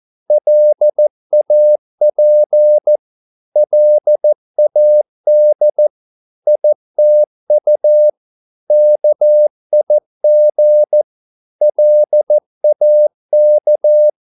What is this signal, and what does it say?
Morse code, heard as a keyed tone, about 14 wpm, 600 Hz. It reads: LAP LAD ITU KIG LAK
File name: Korte ord DK 14wpm